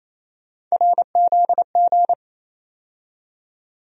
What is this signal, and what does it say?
Morse code, heard as a keyed tone, about 28 wpm, 700 Hz. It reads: RZG
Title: USA Call Sign Postfixes - 3 Letters 28wpm